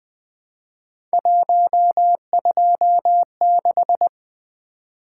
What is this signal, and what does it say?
Morse code, heard as a keyed tone, about 20 wpm, 700 Hz. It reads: 126
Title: Three Letter-Number Combinations - 20wpm